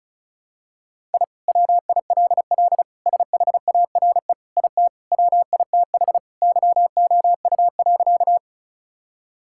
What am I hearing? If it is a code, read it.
Morse code, heard as a keyed tone, about 34 wpm, 700 Hz. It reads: I WILL SHARE IT WITH YOU.
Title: Sentences from Top 1000 Words - Review Part 6 - 35wpm